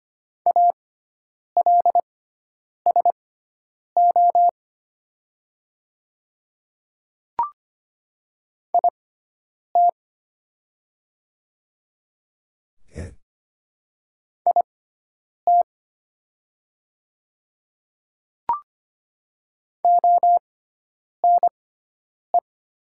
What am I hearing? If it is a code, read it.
Morse code, heard as a keyed tone, about 25 wpm, spaced as for 9 wpm, 700 Hz. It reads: ALSO IT IT ONE
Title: 6x Character Spacing - Top 200 Words - 25wpm